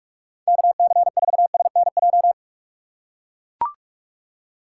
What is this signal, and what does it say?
Morse code, heard as a keyed tone, about 44 wpm, 690 Hz. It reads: KC4SNJ
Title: USA Call Signs - 6 Characters - 2x Character Spacing - 45wpm